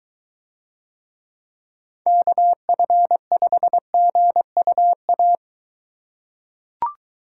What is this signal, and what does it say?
Morse code, heard as a keyed tone, about 22 wpm, 700 Hz. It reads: KF5GUA